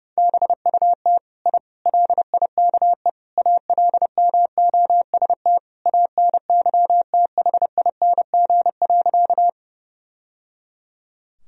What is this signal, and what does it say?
Morse code, heard as a keyed tone, about 30 wpm, 700 Hz. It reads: BUT I LIKE ALMOST ANYTHING.
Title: Sentences from Top 1000 Words - Review Part 4 - 30wpm